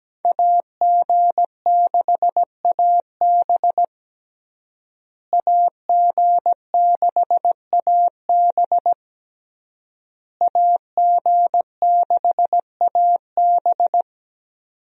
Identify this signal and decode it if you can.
Morse code, heard as a keyed tone, about 17 wpm, 700 Hz. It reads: AG6AB AG6AB AG6AB
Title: USA Call Signs - 5 Characters - Repeated 3x - 17wpm